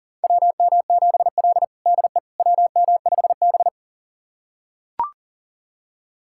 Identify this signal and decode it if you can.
Morse code, heard as a keyed tone, about 40 wpm, 700 Hz. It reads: WM7L DE WM5B
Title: USA Callsign DE Callsign - 4 Characters - 40wpm